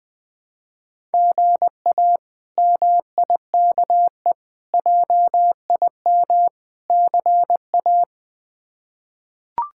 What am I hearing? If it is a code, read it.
Morse code, heard as a keyed tone, about 20 wpm, 700 Hz. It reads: GA MIKE JIM CA
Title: K1USN SST QSO Practice - 20wpm